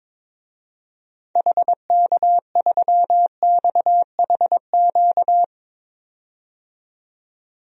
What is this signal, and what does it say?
Morse code, heard as a keyed tone, about 22 wpm, 700 Hz. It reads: HK3XHQ
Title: DXCC-100 Call Signs - 6 Characters - 22wpm